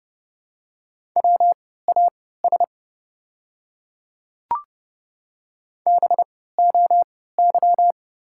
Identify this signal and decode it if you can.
Morse code, heard as a keyed tone, about 31 wpm, spaced as for 17 wpm, 690 Hz. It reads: WAS BOY
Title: 3x Character Spacing - Top 300 Words - 30wpm